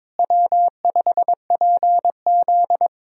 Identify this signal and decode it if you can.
Morse code, heard as a keyed tone, about 22 wpm, 700 Hz. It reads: W5PZ
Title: USA Call Signs - 4 Characters - Repeated 3x - 22wpm